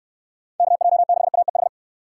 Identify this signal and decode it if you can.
Morse code, heard as a keyed tone, about 69 wpm, 700 Hz. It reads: KJ6AEH